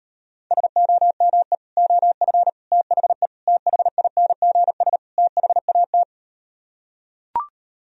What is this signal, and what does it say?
Morse code, heard as a keyed tone, about 38 wpm, 700 Hz. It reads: SOME OF THE THINGS THAT